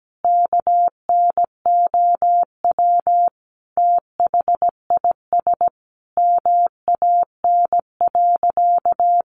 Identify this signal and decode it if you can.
Morse code, heard as a keyed tone, about 17 wpm, 700 Hz. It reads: KNOW THIS MAN.
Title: Sentences from Top 600 Words 17wpm